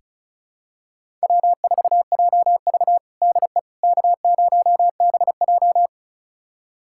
Morse code, heard as a keyed tone, about 34 wpm, 700 Hz. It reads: W4JV DE K0BJ
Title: USA Callsign DE Callsign - 4 Characters - 35wpm